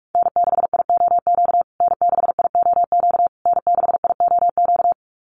Morse code, heard as a keyed tone, about 46 wpm, 690 Hz. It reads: N6IOQ N6IOQ N6IOQ
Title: USA Call Signs Repeated 3x 45wpm